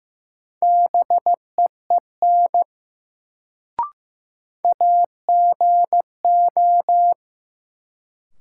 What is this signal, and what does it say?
Morse code, heard as a keyed tone, about 15 wpm, 700 Hz. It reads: BEEN AGO
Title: Sets of 2 Words - New Word Drill - 15wpm